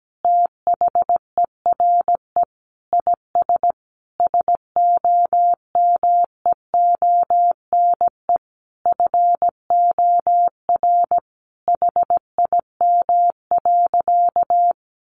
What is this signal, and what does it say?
Morse code, heard as a keyed tone, about 17 wpm, 700 Hz. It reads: THERE IS SOMEONE FOR HIM.
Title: Sentences from Top 600 Words 17wpm